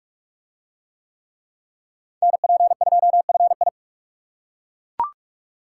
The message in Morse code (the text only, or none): NP2FI